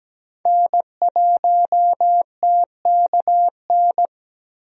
Morse code, heard as a keyed tone, about 17 wpm, 700 Hz. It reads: N1TKN
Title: USA Call Signs - 5 Characters - Repeated 2x - 17wpm